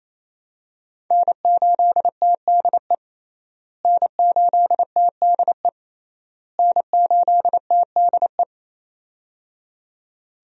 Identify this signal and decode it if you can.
Morse code, heard as a keyed tone, about 28 wpm, 700 Hz. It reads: N8TDE N8TDE N8TDE
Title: USA Call Signs - 5 Characters - Repeated 3x - 28wpm